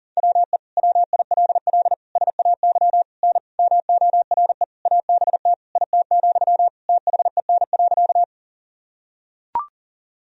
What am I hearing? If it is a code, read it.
Morse code, heard as a keyed tone, about 40 wpm, 700 Hz. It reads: WE WILL SAY N MORE ABT IT, THEN.
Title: Sentences from Top 200 Words and Abbreviations 40wpm